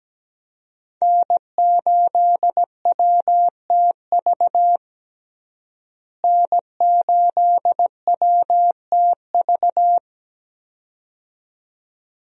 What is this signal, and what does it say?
Morse code, heard as a keyed tone, about 17 wpm, 700 Hz. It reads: N8WTV N8WTV
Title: USA Call Signs - 5 Characters - Repeated 2x - 17wpm